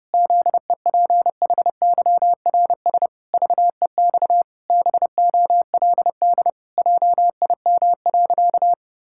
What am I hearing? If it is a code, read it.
Morse code, heard as a keyed tone, about 30 wpm, 700 Hz. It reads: ZEPHYRS VEX BOLD JIM.
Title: Pangram Challenge v2 - 30wpm